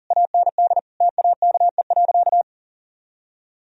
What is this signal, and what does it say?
Morse code, heard as a keyed tone, about 40 wpm, 700 Hz. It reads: AND TAKE.
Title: Sentences from Top 1000 Words - Review Part 4 - 40wpm